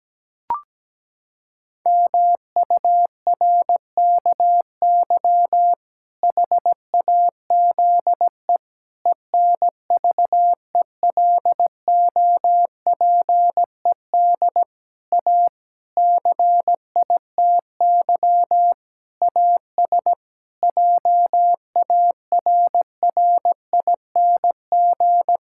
Morse code, heard as a keyed tone, about 17 wpm, 700 Hz. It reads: MURKY HAZE ENVELOPED A CITY AS JARRING
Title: Pangram Challenge v2 - 17wpm